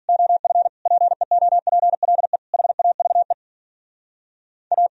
Morse code, heard as a keyed tone, about 51 wpm, 700 Hz. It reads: OF PEOPLE HAVE A